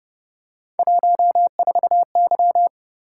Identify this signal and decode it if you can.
Morse code, heard as a keyed tone, about 30 wpm, 700 Hz. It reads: 14Y